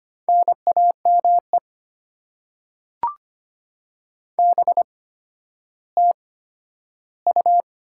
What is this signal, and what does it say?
Morse code, heard as a keyed tone, about 25 wpm, spaced as for 12 wpm, 700 Hz. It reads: NAME BTU
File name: Top 90 QSO Elements - 8x Character Spacing - 25wpm